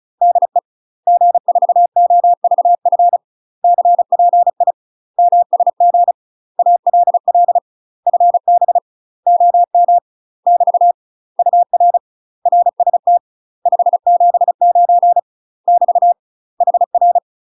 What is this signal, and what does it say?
Morse code, heard as a keyed tone, about 36 wpm, 700 Hz. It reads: DE G4OVF CPI MSG ALL FB OM <BT> UR RST 579 <BT> HR
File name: UK MASTER QSOs TRAINING / 35WPM / UK MASTER QSOs 0010